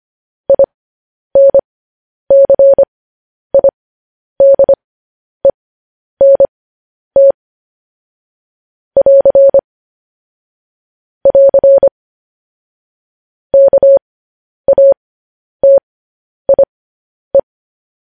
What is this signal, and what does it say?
Morse code, heard as a keyed tone, about 25 wpm, spaced as for 10 wpm, 550 Hz. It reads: INCIDENT <AR> <AR> KATIE
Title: News Headlines at 10